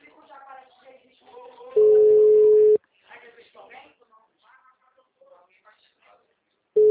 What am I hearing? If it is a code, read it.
Ringback tone (Europe, 425 Hz), with speech.
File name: out- -2015- - - .373